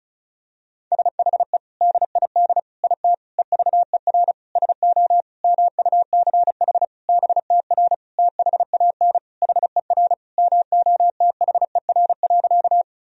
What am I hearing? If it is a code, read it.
Morse code, heard as a keyed tone, about 36 wpm, 700 Hz. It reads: SHE DID IT EVER SO MUCH BTR THAN HER MOTHER.
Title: Sentences from Top 400 Words and Abbreviations 35wpm